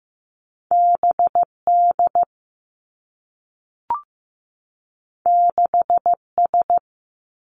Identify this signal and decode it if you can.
Morse code, heard as a keyed tone, about 15 wpm, 700 Hz. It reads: B D 6 S
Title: Two Challenging Letter-Number Combinations 15wpm